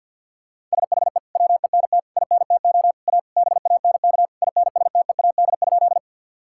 Morse code, heard as a keyed tone, about 50 wpm, 700 Hz. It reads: SHE WENT INTO A BANK INSTEAD?
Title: Sentences from Top 1000 Words - Review Part 6 - 50wpm